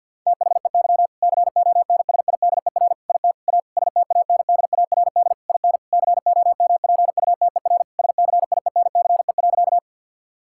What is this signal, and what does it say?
Morse code, heard as a keyed tone, about 50 wpm, 700 Hz. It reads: THEY CONSIDER IT A STANDARD IN COMPUTER SCIENCE.
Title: Sentences from Top 1000 Words - Review Part 6 - 50wpm